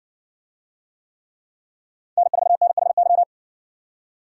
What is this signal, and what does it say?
Morse code, heard as a keyed tone, about 60 wpm, 700 Hz. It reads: N4NHY